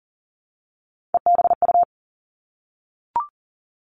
Morse code, heard as a keyed tone, about 42 wpm, 700 Hz. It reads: EBU
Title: Three Challenging Letter-Number Combinations 40wpm